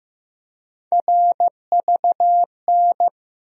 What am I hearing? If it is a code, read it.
Morse code, heard as a keyed tone, about 15 wpm, 700 Hz. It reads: RVN
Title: USA Call Sign Postfixes - 3 Letters 15wpm